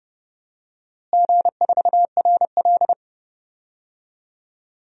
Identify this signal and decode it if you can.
Morse code, heard as a keyed tone, about 30 wpm, 700 Hz. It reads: G4RL
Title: DXCC-100 Call Signs - 4 Characters - Repeated 3x - 30wpm